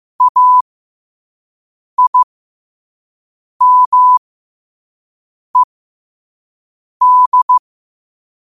Morse code, heard as a keyed tone, about 15 wpm, spaced as for 5 wpm, 1000 Hz. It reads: AIMED
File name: qotdfiles / 5wpm